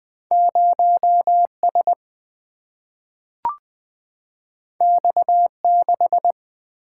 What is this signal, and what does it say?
Morse code, heard as a keyed tone, about 20 wpm, 700 Hz. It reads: 0S X6
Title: Single-Letter Single-Number 20wpm